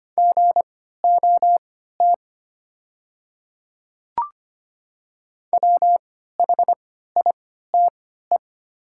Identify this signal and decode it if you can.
Morse code, heard as a keyed tone, about 25 wpm, spaced as for 14 wpm, 700 Hz. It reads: GOT WHITE
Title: 3x Character Spacing - Top 300 Words - 25wpm